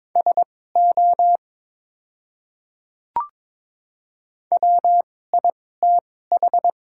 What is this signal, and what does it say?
Morse code, heard as a keyed tone, about 22 wpm, spaced as for 16 wpm, 700 Hz. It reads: SO WITH
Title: 2x Character Spacing - Top 300 Words - 22wpm